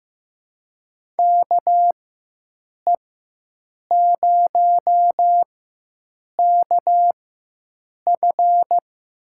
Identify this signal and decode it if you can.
Morse code, heard as a keyed tone, about 15 wpm, spaced as for 7 wpm, 700 Hz. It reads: KE0KF
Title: USA Call Signs - 6 Characters - 4x Character Spacing - 15wpm